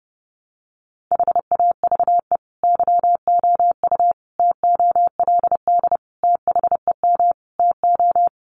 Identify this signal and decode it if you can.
Morse code, heard as a keyed tone, about 30 wpm, 700 Hz. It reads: HAVE YOU TOLD THEM TO
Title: Sentences from Top 500 Words 30wpm